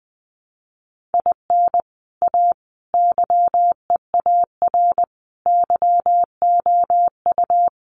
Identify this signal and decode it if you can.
Morse code, heard as a keyed tone, about 19 wpm, 700 Hz. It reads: IN A YEAR YOU
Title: Sentences from Top 600 Words 20wpm